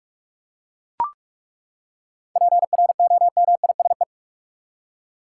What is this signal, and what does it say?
Morse code, heard as a keyed tone, about 41 wpm, 700 Hz. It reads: PROMISE
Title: Seven Letter Words - 45wpm